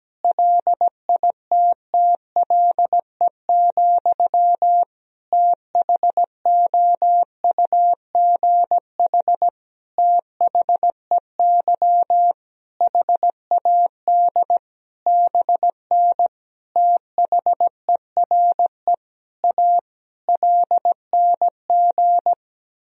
Morse code, heard as a keyed tone, about 17 wpm, 700 Hz. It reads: LITTLE, THOUGH THEY HAD BN THERE A LNG
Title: Sentences from Top 400 Words and Abbreviations 17wpm